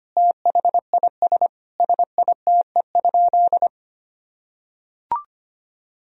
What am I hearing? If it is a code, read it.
Morse code, heard as a keyed tone, about 25 wpm, 700 Hz. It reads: THIS SITE?
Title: Sentences from Top 1000 Words - Review Part 4 - 25wpm